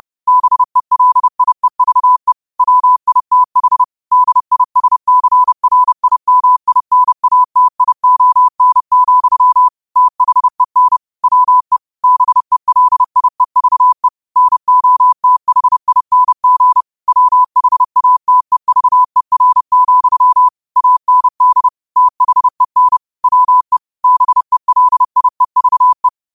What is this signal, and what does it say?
Morse code, heard as a keyed tone, about 30 wpm, 1000 Hz. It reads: BELIEVE WITH DISCRIMINATION, THEN WE BELIEVE NOTHING WHATEVER, AND THEN WE BELIEVE